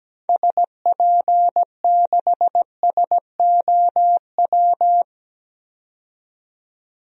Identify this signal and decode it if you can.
Morse code, heard as a keyed tone, about 17 wpm, 700 Hz. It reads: SP6SOW
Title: DXCC-100 Call Signs - 6 Characters - 17wpm